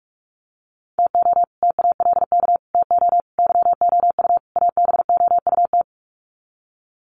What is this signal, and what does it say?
Morse code, heard as a keyed tone, about 44 wpm, 700 Hz. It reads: TO TALK TO YOU ABOUT
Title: Sets of 5 Words Encore 45wpm